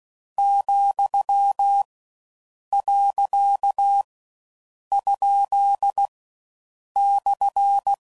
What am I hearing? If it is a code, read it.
Morse code, heard as a keyed tone, about 16 wpm, 800 Hz. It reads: , . ? /
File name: Mixed / Misc / Words